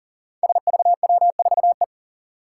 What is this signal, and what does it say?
Morse code, heard as a keyed tone, about 40 wpm, 700 Hz. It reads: SVW4E
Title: Five Challenging Letter-Number Combinations 40wpm